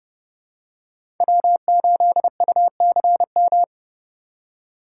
Morse code, heard as a keyed tone, about 30 wpm, 700 Hz. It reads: W8UCM
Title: USA Call Signs - 5 Characters - 4x Character Spacing - 30wpm